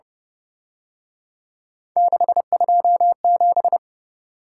Morse code, heard as a keyed tone, about 31 wpm, 700 Hz. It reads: 627
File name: Speed-Racing - Three Letter-Number Combinations - 25wpm